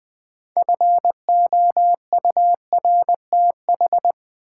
Morse code, heard as a keyed tone, about 20 wpm, 690 Hz. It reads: FOURTH
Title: Build A Word - 2 Letters At A Time - 6-Letter Words - 20wpm